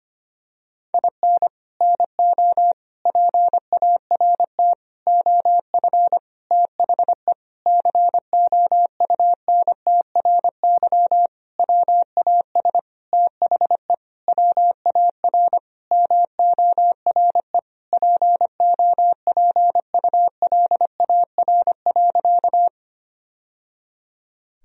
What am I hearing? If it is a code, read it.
Morse code, heard as a keyed tone, about 25 wpm, 700 Hz. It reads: IN NO PART OF THE COUNTRY WAS THE WAR MORE POPULAR.
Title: Sentences from Top 1000 Words - Review Part 3 - 25wpm